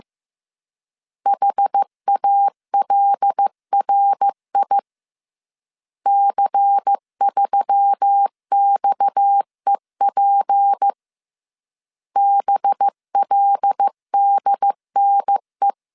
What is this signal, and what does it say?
Morse code, heard as a keyed tone, about 15 wpm, 800 Hz. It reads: HALRI C3XEP BLDNE